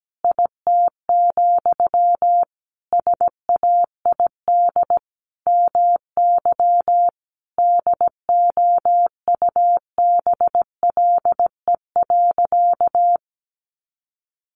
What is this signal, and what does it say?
Morse code, heard as a keyed tone, about 17 wpm, 700 Hz. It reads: IT, SAID MY DOUBLE.
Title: Sentences from Top 1400 Words 17wpm